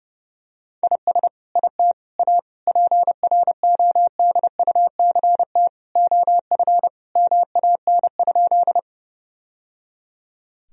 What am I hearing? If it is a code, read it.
Morse code, heard as a keyed tone, about 30 wpm, 700 Hz. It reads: IS IT A PRODUCT OF MAN?
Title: Sentences from Top 1000 Words - Review Part 2 - 30wpm